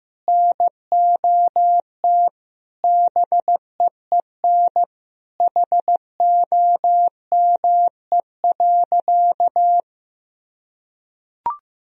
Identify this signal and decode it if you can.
Morse code, heard as a keyed tone, about 15 wpm, 700 Hz. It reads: NOT BEEN HOME.
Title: Sentences from Top 1000 Words - Review Part 1 - 15wpm